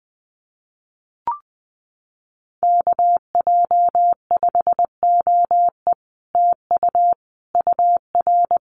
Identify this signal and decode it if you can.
Morse code, heard as a keyed tone, about 20 wpm, 700 Hz. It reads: KJ5OE TU UR
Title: Common QSO Phrases Encore 20wpm